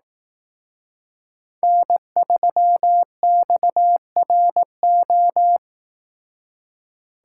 Morse code, heard as a keyed tone, about 18 wpm, 700 Hz. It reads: N3XRO
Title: Speed Racing - USA 1x3 Call Signs 15wpm